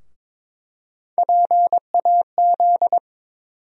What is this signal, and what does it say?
Morse code, heard as a keyed tone, about 22 wpm, 700 Hz. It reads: PAZ
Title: USA Call Sign Postfixes - 3 Letters 22wpm